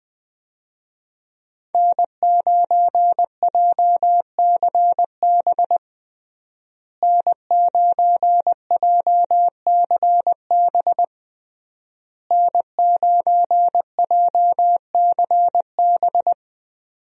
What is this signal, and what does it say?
Morse code, heard as a keyed tone, about 20 wpm, 700 Hz. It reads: N9JCB N9JCB N9JCB
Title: USA Call Signs - 5 Characters - Repeated 3x - 20wpm